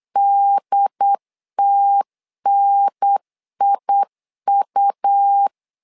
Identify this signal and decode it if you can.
Morse code, heard as a keyed tone, about 8 wpm, 800 Hz. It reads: DTNIU
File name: LEZIONI / 3 MWRO / 3 M 8wpm 2300